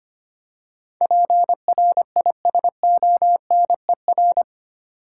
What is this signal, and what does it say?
Morse code, heard as a keyed tone, about 25 wpm, 700 Hz. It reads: PRISONER
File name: 8 Letter Words - First 3 Letters Given - 25wpm